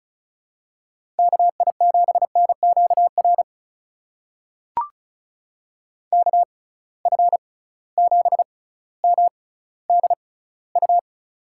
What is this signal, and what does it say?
Morse code, heard as a keyed tone, about 36 wpm, spaced as for 18 wpm, 700 Hz. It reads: KI7NQR KF7MDU
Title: USA Call Signs - 6 Characters - 6x Character Spacing - 35wpm